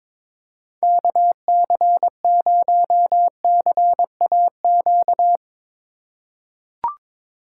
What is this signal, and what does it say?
Morse code, heard as a keyed tone, about 22 wpm, 700 Hz. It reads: KC0CAQ
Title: USA Call Signs - 6 Characters - 4x Character Spacing - 22wpm